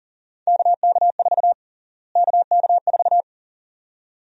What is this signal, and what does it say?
Morse code, heard as a keyed tone, about 40 wpm, 700 Hz. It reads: KK4 KK4
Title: USA Call Sign Prefixes - Repeated 2x - 40wpm